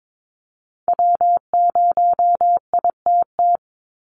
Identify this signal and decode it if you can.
Morse code, heard as a keyed tone, about 22 wpm, 700 Hz. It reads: W0ITT
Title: Call Signs Repeated 2x - 22wpm